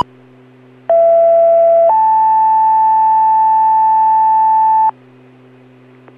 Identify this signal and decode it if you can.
Two-tone page, heard as a one-second tone, then a three-second tone, about 670 then 880 Hz. It is heard from 0.9 to 4.9 s.